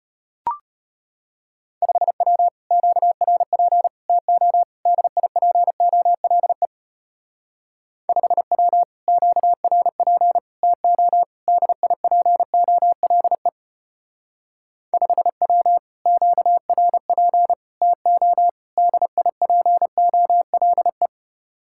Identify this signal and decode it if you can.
Morse code, heard as a keyed tone, about 31 wpm, 700 Hz. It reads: 5W QRP TO DIPOLE 5W QRP TO DIPOLE 5W QRP TO DIPOLE
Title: Speed-Racing Common QSO Phrases - 25wpm